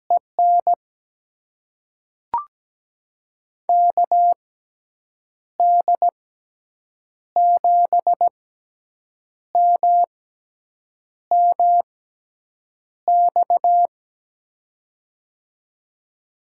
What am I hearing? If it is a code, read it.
Morse code, heard as a keyed tone, about 17 wpm, spaced as for 7 wpm, 700 Hz. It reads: EN KD7MMX
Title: USA Call Signs - 6 Characters - 6x Character Spacing - 17wpm